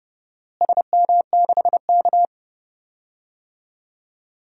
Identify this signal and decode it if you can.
Morse code, heard as a keyed tone, about 30 wpm, 700 Hz. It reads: SM6K
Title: DXCC-100 Call Signs - 4 Characters - Repeated 3x - 30wpm